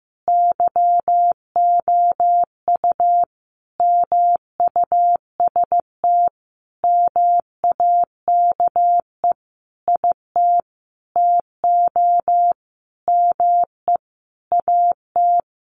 Morse code, heard as a keyed tone, about 15 wpm, 700 Hz. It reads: YOU MUST MAKE IT TO ME AT
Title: Sentences from Top 500 Words 15wpm